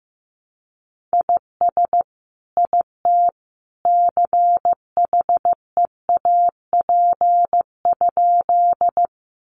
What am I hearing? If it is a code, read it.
Morse code, heard as a keyed tone, about 15 wpm, 700 Hz. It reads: IS IT CHEAP?
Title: Sentences from Top 1200 Words 15wpm